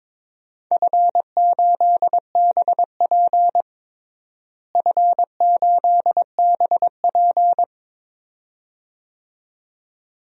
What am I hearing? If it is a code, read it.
Morse code, heard as a keyed tone, about 22 wpm, 700 Hz. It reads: F8BP F8BP
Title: DXCC-100 Call Signs - 4 Characters - Repeated 2x - 22wpm